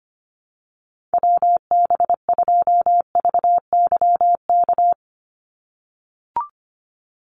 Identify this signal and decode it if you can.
Morse code, heard as a keyed tone, about 25 wpm, 700 Hz. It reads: WB2VYK
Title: Call Signs - 25wpm